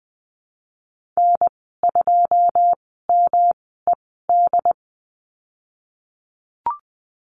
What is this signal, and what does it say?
Morse code, heard as a keyed tone, about 20 wpm, spaced as for 14 wpm, 700 Hz. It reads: N2MED
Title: 173 Lesson 28g - Number 9 - Call Signs 20wpm